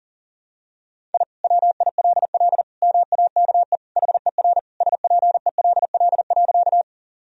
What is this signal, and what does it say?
Morse code, heard as a keyed tone, about 40 wpm, 700 Hz. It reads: I WILL MAKE HER SPELL.
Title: Sentences from Top 1000 Words - Review Part 1 - 40wpm